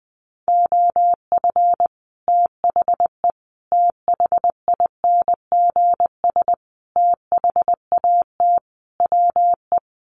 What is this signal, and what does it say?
Morse code, heard as a keyed tone, about 20 wpm, 700 Hz. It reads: OF THE THINGS THAT WE
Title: Sets of 5 Words Encore 20wpm